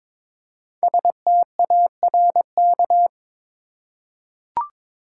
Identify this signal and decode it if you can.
Morse code, heard as a keyed tone, about 22 wpm, 700 Hz. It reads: STARK
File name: 5 Letter Words - First 3 Letters Given - 22wpm